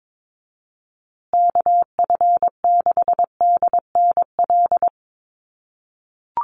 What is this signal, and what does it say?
Morse code, heard as a keyed tone, about 22 wpm, 700 Hz. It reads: KF6DNL
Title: USA Call Signs Repeated 3x 22wpm